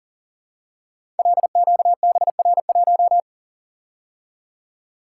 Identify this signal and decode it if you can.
Morse code, heard as a keyed tone, about 38 wpm, 700 Hz. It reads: LQBR1
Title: Code Groups - 5 Letter:Numbers - 40wpm